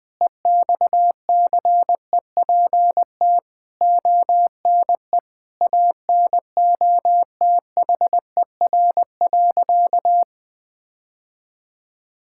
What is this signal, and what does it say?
Morse code, heard as a keyed tone, about 20 wpm, 700 Hz. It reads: EXCEPT ONE ANOTHER.
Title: Sentences from Top 1600 Words 20wpm